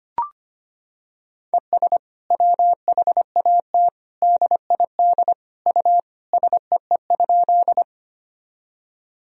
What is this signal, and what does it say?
Morse code, heard as a keyed tone, about 25 wpm, 700 Hz. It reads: ES WHAT DID U SEE?
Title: Sentences from Top 100 Words and Abbreviations 25wpm